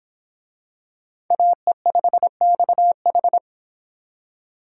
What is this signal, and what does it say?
Morse code, heard as a keyed tone, about 26 wpm, 700 Hz. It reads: AE5XH